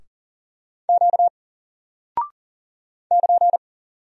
Morse code, heard as a keyed tone, about 40 wpm, 700 Hz. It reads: Q <KN>
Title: Single Character (Letter:Number:Punctuation:Prosign) 40wpm